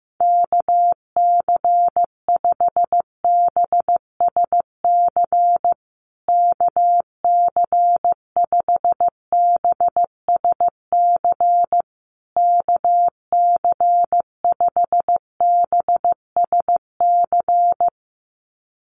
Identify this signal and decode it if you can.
Morse code, heard as a keyed tone, about 15 wpm, 700 Hz. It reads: KC5BSC KC5BSC KC5BSC
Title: USA Call Signs Repeated 3x 15wpm